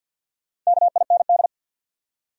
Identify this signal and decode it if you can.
Morse code, heard as a keyed tone, about 68 wpm, spaced as for 58 wpm, 700 Hz. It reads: KIND
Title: Speed-Racing Top 500 Words 50wpm